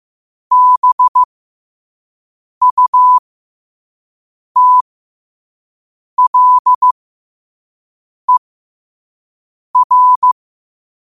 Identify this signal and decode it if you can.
Morse code, heard as a keyed tone, about 15 wpm, spaced as for 5 wpm, 1000 Hz. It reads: BUTLER
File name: qotdfiles / 5wpm